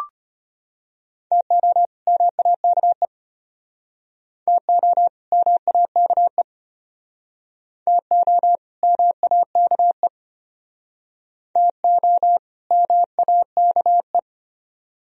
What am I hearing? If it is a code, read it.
Morse code, heard as a keyed tone, about 31 wpm, 700 Hz. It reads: TO MAKE TO MAKE TO MAKE TO MAKE
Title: Speed-Racing - Sets of 2 Words 25wpm